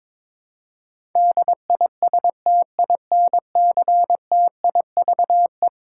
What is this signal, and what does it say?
Morse code, heard as a keyed tone, about 22 wpm, 700 Hz. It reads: DISTINCTIVE
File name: practice / 22wpm / Eleven Letter Words - 22wpm